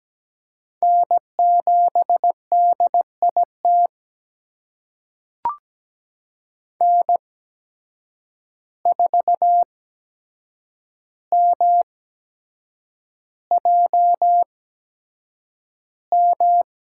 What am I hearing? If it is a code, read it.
Morse code, heard as a keyed tone, about 17 wpm, spaced as for 7 wpm, 700 Hz. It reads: N7DIT N4MJM
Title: USA Call Signs - 8x Character Spacing - 17wpm